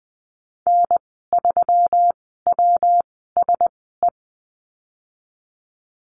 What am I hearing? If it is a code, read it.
Morse code, heard as a keyed tone, about 20 wpm, spaced as for 15 wpm, 700 Hz. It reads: N3WSE
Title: 140 Lesson 23g - Letter f - Call Signs 20wpm